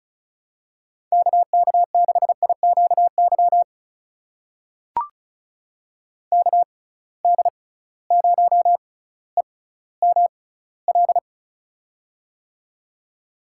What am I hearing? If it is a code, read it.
Morse code, heard as a keyed tone, about 36 wpm, spaced as for 18 wpm, 700 Hz. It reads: KK6IQY KD0EML
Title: USA Call Signs - 6 Characters - 6x Character Spacing - 35wpm